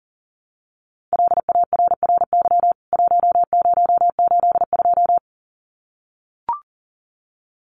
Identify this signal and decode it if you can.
Morse code, heard as a keyed tone, about 40 wpm, 700 Hz. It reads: LARRY 1082
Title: CWT Exchange 40wpm